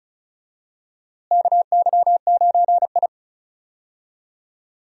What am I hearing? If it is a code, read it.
Morse code, heard as a keyed tone, about 36 wpm, 700 Hz. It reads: KY9I
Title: USA Call Signs - 4 Characters - Repeated 3x - 35wpm